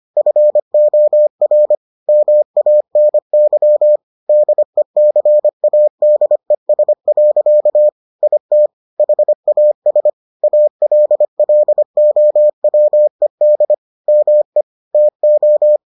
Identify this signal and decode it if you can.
Morse code, heard as a keyed tone, about 25 wpm, 600 Hz. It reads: FOR MANY DECADES. IT HAS ALLOWED ME TO